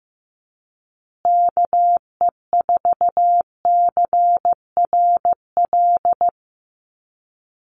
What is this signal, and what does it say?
Morse code, heard as a keyed tone, about 15 wpm, 700 Hz. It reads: KE4CRL
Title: USA Call Signs Repeated 3x 15wpm